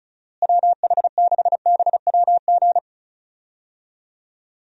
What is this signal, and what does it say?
Morse code, heard as a keyed tone, about 34 wpm, 700 Hz. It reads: WH6BWG
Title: USA Call Signs - 6 Characters - Repeated 3x - 35wpm